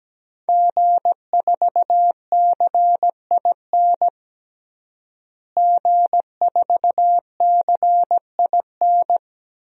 Morse code, heard as a keyed tone, about 17 wpm, 700 Hz. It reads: G4CIN G4CIN
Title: DXCC-100 Call Signs - 5 Characters - Repeated 2x - 17wpm